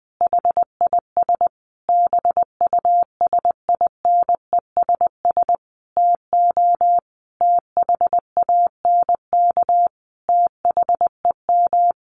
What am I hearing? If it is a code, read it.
Morse code, heard as a keyed tone, about 19 wpm, 700 Hz. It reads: HIS BUSINESS TO THANK THEM
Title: Sentences from Top 1400 Words 20wpm